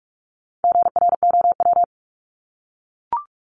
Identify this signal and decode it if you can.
Morse code, heard as a keyed tone, about 44 wpm, 700 Hz. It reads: GROW
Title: ING Suffix - Top 100 Words with Base Words 45wpm